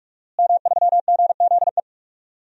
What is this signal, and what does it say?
Morse code, heard as a keyed tone, about 46 wpm, 700 Hz. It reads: M3GZE
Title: DXCC-100 Call Signs - 5 Characters - Repeated 2x - 45wpm